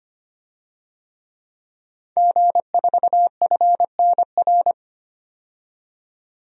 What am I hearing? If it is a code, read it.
Morse code, heard as a keyed tone, about 25 wpm, 700 Hz. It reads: G4FNR
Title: DXCC-100 Call Signs - 5 Characters - 25wpm